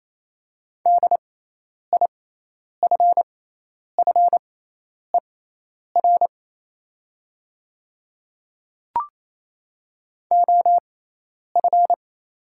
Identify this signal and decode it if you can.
Morse code, heard as a keyed tone, about 28 wpm, spaced as for 10 wpm, 700 Hz. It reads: DIFFER OF
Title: 6x Character Spacing - Top 200 Words - 28wpm